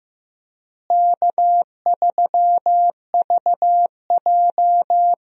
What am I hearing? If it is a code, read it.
Morse code, heard as a keyed tone, about 16 wpm, 700 Hz. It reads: K3VJ
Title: USA Call Signs - 4 Characters - Repeated 3x - 15wpm